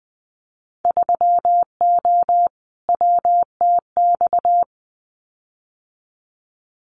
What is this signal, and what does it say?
Morse code, heard as a keyed tone, about 20 wpm, 700 Hz. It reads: 3O WTX
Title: Winter Field Day Exchanges Encore 20wpm